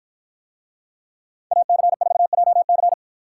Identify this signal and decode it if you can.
Morse code, heard as a keyed tone, about 49 wpm, 700 Hz. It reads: AC4JC